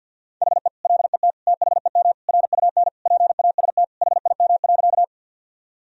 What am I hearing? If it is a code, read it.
Morse code, heard as a keyed tone, about 50 wpm, 700 Hz. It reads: HE LET THEM RUN PAST HIM.
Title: Sentences from Top 1000 Words - Review Part 3 - 50wpm